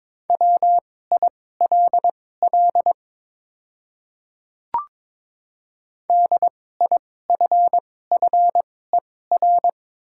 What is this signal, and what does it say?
Morse code, heard as a keyed tone, about 22 wpm, spaced as for 16 wpm, 700 Hz. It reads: WILL DIFFER
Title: 2x Character Spacing - Top 300 Words - 22wpm